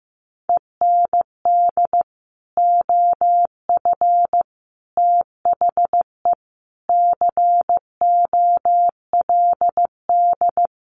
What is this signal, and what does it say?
Morse code, heard as a keyed tone, about 14 wpm, 700 Hz. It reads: END OF THE COLD
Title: Sets of 5 Words Encore 15wpm